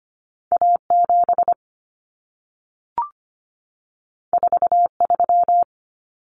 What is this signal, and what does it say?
Morse code, heard as a keyed tone, about 25 wpm, 700 Hz. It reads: A7 43
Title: Common Cut Numbers - 2-Digit 25wpm